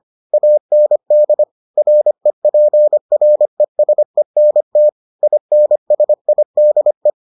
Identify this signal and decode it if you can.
Morse code, heard as a keyed tone, about 25 wpm, 600 Hz. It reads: AND REPRESENT INSIDE